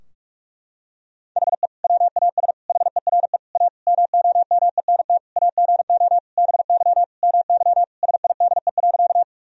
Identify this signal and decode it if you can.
Morse code, heard as a keyed tone, about 46 wpm, 700 Hz. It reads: HE WAS HERE A MOMENT AGO BY MY SIDE.
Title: Sentences from Top 1000 Words - Review Part 3 - 45wpm